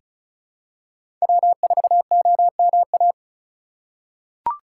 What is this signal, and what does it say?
Morse code, heard as a keyed tone, about 36 wpm, 700 Hz. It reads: W4OMA
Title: USA Call Signs - 3x Character Spacing - 35wpm